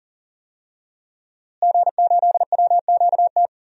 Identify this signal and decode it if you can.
Morse code, heard as a keyed tone, about 40 wpm, 700 Hz. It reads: G8WQT
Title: DXCC-100 Call Signs - 5 Characters - Repeated 2x - 40wpm